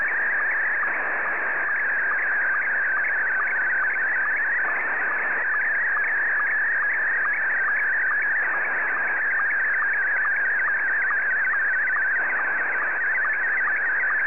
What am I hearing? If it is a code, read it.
SSTV (slow-scan television), Scottie 1.